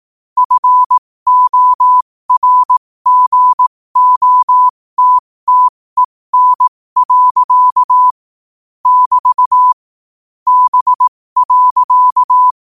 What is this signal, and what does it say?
Morse code, heard as a keyed tone, about 18 wpm, spaced as for 15 wpm, 1000 Hz. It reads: FORGOTTEN. <BT> B.
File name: 15wpm